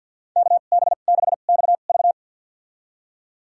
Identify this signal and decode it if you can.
Morse code, heard as a keyed tone, about 50 wpm, spaced as for 36 wpm, 700 Hz. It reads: KB6XV
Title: USA Call Signs - 2x Character Spacing - 50wpm